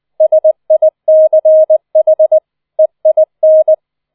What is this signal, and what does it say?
Morse code, heard as a keyed tone, about 20 wpm, 630 Hz. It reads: SICH EIN
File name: 004 Vierter Streich 20wpm